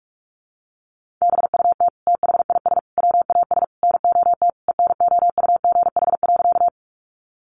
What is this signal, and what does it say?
Morse code, heard as a keyed tone, about 44 wpm, 700 Hz. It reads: BUT THIS WAS NOT ENOUGH.
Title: Sentences from Top 600 Words 45wpm